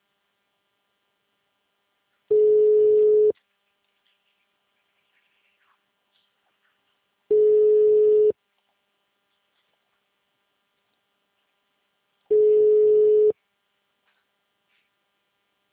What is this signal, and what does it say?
Ringback tone (Europe, 425 Hz).